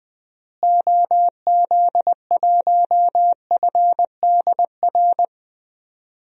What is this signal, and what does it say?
Morse code, heard as a keyed tone, about 20 wpm, 700 Hz. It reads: OZ1FDR
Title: DXCC-100 Call Signs - 6 Characters - Repeated 2x - 20wpm